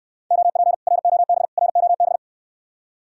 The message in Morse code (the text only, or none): OP ROB ROB